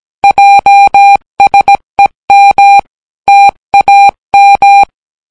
Morse code, heard as a keyed tone, about 17 wpm, 800 Hz. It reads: JSEM TAM